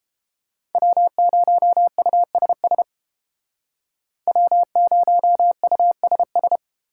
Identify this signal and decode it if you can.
Morse code, heard as a keyed tone, about 31 wpm, 700 Hz. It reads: W0USS W0USS